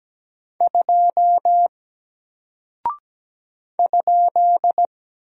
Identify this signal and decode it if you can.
Morse code, heard as a keyed tone, about 17 wpm, 700 Hz. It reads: 2 ?
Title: Single Character (Letter:Number:Punctuation:Prosign) Alt 17wpm